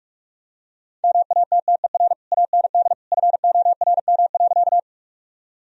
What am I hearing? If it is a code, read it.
Morse code, heard as a keyed tone, about 46 wpm, 700 Hz. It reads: MATTER AND FORM.
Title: Sentences from Top 1000 Words - Review Part 1 - 45wpm